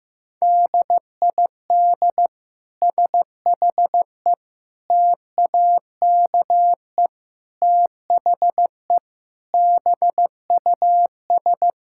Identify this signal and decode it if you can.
Morse code, heard as a keyed tone, about 15 wpm, 700 Hz. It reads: DID SHE TAKE THE BUS
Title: Sentences from Top 1000 Words - Review Part 6 - 15wpm